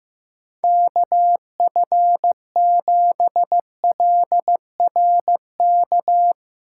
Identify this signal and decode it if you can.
Morse code, heard as a keyed tone, about 16 wpm, 700 Hz. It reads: KF7LRK
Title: Speed Racing - USA 2x3 Call Signs 15wpm